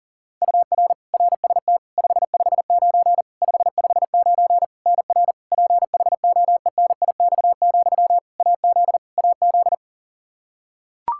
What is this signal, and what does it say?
Morse code, heard as a keyed tone, about 40 wpm, 700 Hz. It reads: UR RST 559 559 NR PHOENIX, AZ AZ
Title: Practice QSO - Exchange 1 - Beginning - 40wpm